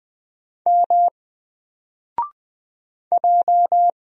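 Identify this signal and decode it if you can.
Morse code, heard as a keyed tone, about 20 wpm, 700 Hz. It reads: M J
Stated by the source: Single letter and number Morse drill at 20 wpm, from N4LCD.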